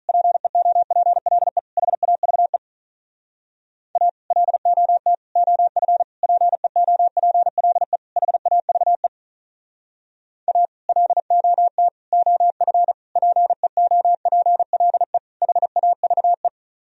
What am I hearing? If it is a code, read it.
Morse code, heard as a keyed tone, about 46 wpm, 700 Hz. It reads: PEOPLE HAVE A LOT OF PEOPLE HAVE A LOT OF PEOPLE HAVE